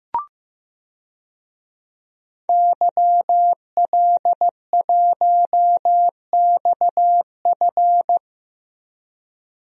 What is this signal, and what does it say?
Morse code, heard as a keyed tone, about 15 wpm, 700 Hz. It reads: YL1XF
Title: DXCC-100 Call Signs - 5 Characters - Repeated 3x - 15wpm